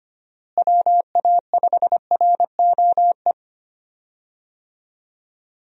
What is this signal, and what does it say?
Morse code, heard as a keyed tone, about 25 wpm, 700 Hz. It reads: WA5ROE
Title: USA Call Signs - 6 Characters - Repeated 3x - 25wpm